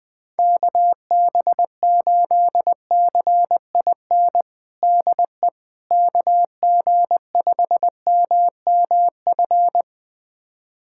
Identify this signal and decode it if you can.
Morse code, heard as a keyed tone, about 20 wpm, 700 Hz. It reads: KB8CIN DE KG5MMF
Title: USA Callsign DE Callsign - 6 Characters - 20wpm